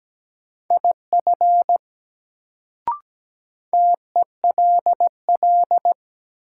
Morse code, heard as a keyed tone, about 17 wpm, 700 Hz. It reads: IF TELL
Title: Sets of 2 Words - New Word Drill - 17wpm